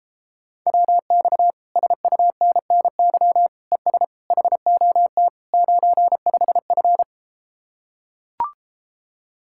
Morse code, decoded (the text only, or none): WX SUNNY ES HOT 95F